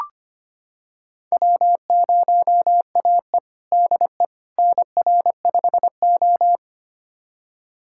Morse code, heard as a keyed tone, about 25 wpm, 700 Hz. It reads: W0AE DE NR5O
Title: USA Callsign DE Callsign - 4 Characters - 25wpm